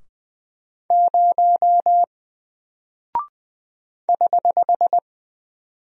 Morse code, heard as a keyed tone, about 20 wpm, 700 Hz. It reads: 0 <HH>
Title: Single Character (Letter:Number:Punctuation:Prosign) Alt 20wpm